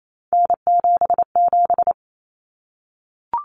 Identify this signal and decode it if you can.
Morse code, heard as a keyed tone, about 28 wpm, 700 Hz. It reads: N77
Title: Common Cut Numbers - 3-Digit 28wpm